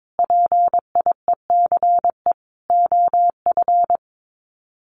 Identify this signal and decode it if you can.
Morse code, heard as a keyed tone, about 22 wpm, 700 Hz. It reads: PIECE OF
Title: Sets of 3 Words 22wpm